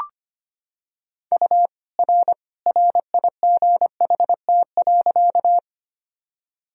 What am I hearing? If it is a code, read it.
Morse code, heard as a keyed tone, about 25 wpm, 700 Hz. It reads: U R RIGHT.
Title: Sentences from Top 200 Words and Abbreviations 25wpm